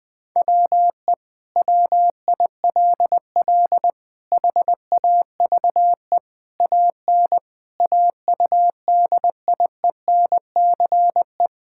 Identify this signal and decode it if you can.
Morse code, heard as a keyed tone, about 20 wpm, 700 Hz. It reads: WE WILL HAVE AN AUDIENCE
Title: Sentences from Top 1000 Words - Review Part 5 - 20wpm